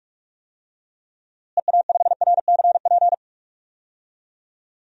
Morse code, heard as a keyed tone, about 46 wpm, 700 Hz. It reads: EA5RCP
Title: DXCC-100 Call Signs - 6 Characters - Repeated 3x - 45wpm